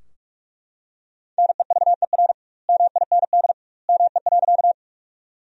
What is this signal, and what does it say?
Morse code, heard as a keyed tone, about 41 wpm, 700 Hz. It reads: NEVER MIND ME.
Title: 2x Word-Spacing - Sentences from Top 500 Words - 45wpm